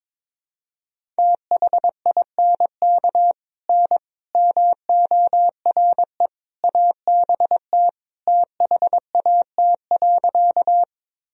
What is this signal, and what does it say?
Morse code, heard as a keyed tone, about 22 wpm, 700 Hz. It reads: THINK N MORE ABT THAT.
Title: Sentences from Top 200 Words and Abbreviations 22wpm